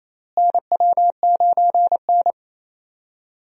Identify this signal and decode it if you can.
Morse code, heard as a keyed tone, about 28 wpm, 700 Hz. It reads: NW9N
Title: USA Call Signs - 4 Characters - Repeated 3x - 28wpm